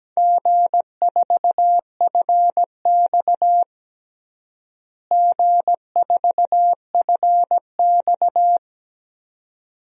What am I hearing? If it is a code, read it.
Morse code, heard as a keyed tone, about 17 wpm, 700 Hz. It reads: G4FX G4FX
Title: DXCC-100 Call Signs - 4 Characters - Repeated 3x - 17wpm